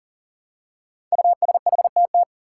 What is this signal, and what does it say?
Morse code, heard as a keyed tone, about 40 wpm, 700 Hz. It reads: USHTT
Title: Five Challenging Letter-Number Combinations 40wpm